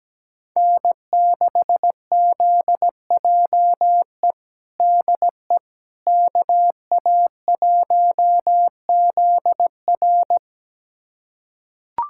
Morse code, heard as a keyed tone, about 17 wpm, 700 Hz. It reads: N6ZJE DE KA1ZR
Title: USA Callsign DE Callsign - 5 Characters - 17wpm